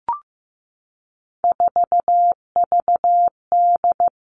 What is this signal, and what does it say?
Morse code, heard as a keyed tone, about 15 wpm, 700 Hz. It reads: 4VD
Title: All Cut Numbers - 3-Digit 15wpm